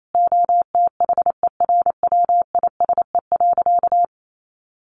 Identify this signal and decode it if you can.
Morse code, heard as a keyed tone, about 28 wpm, 700 Hz. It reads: OTHERWISE.
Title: Sentences from Top 1500 Words 28wpm